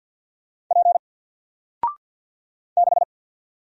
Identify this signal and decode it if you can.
Morse code, heard as a keyed tone, about 50 wpm, 700 Hz. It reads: P 6
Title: Single Letter-Number 50wpm